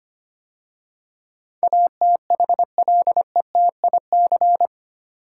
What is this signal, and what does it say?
Morse code, heard as a keyed tone, about 25 wpm, 700 Hz. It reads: ATHLETIC
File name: 8 Letter Words - First 3 Letters Given - 25wpm